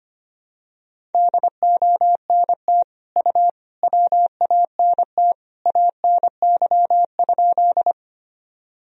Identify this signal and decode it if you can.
Morse code, heard as a keyed tone, about 25 wpm, 700 Hz. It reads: DONT U WANT ANY?
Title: Sentences from Top 300 Words and Abbreviations 25wpm